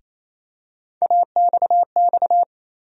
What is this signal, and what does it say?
Morse code, heard as a keyed tone, about 28 wpm, 700 Hz. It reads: AXX
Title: USA Call Sign Postfixes - 3 Letters 28wpm